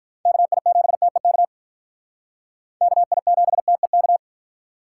Morse code, heard as a keyed tone, about 44 wpm, 700 Hz. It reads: KI7TEK KI7TEK